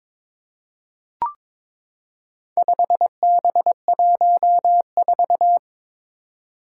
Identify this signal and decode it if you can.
Morse code, heard as a keyed tone, about 22 wpm, 700 Hz. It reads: 5B14
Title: Four Challenging Letter-Number Combinations 22wpm